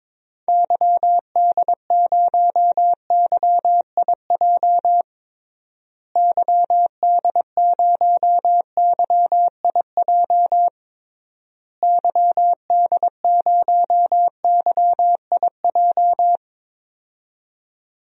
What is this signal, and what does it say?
Morse code, heard as a keyed tone, about 22 wpm, 700 Hz. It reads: YD0YIJ YD0YIJ YD0YIJ
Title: DXCC-100 Call Signs - 6 Characters - Repeated 3x - 22wpm